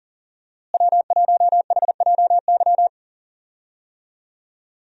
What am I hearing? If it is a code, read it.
Morse code, heard as a keyed tone, about 38 wpm, 700 Hz. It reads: W1HJY
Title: USA Call Signs - 5 Characters - Repeated 2x - 40wpm